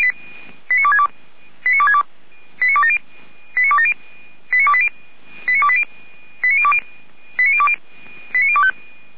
Five-tone selective call (ZVEI), heard as quick runs of five tones, tones of about 70 ms. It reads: 89262 89262 89270 89270 89280 89280 89020 89020 89025
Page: SDR/ZVEI